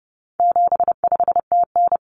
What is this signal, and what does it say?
Morse code, heard as a keyed tone, about 30 wpm, 700 Hz. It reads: 75TN
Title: Common Cut Numbers - 4-Digit 30wpm